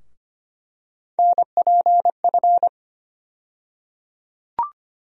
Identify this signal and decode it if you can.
Morse code, heard as a keyed tone, about 25 wpm, 700 Hz. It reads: NPF
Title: USA Call Sign Postfixes - 3 Letters 25wpm